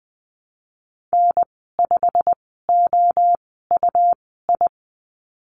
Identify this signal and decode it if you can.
Morse code, heard as a keyed tone, about 20 wpm, spaced as for 15 wpm, 700 Hz. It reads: N5OUI
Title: 128 Lesson 21g - Number 6 - Call Signs 20wpm